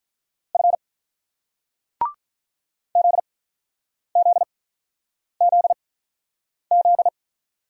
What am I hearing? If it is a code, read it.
Morse code, heard as a keyed tone, about 49 wpm, spaced as for 9 wpm, 700 Hz. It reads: F ZZZZ